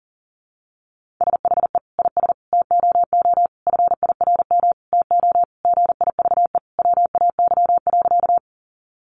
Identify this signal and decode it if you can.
Morse code, heard as a keyed tone, about 40 wpm, 700 Hz. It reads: SHE IS TOO FIRM TO GIVE WAY.
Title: Sentences from Top 1400 Words 40wpm